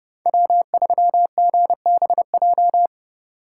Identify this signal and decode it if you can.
Morse code, heard as a keyed tone, about 36 wpm, 700 Hz. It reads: W3GBJ
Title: Speed Racing - USA 1x3 Call Signs 30wpm